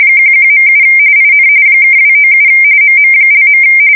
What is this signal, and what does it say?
RTTY, about 45.45 baud. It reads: 11.7 IN (297 MM).
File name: fsk170 / USS-Missouri